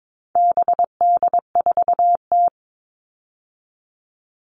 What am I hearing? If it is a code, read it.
Morse code, heard as a keyed tone, about 22 wpm, 700 Hz. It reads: BD4T
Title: All Cut Numbers - 4-Digit 22wpm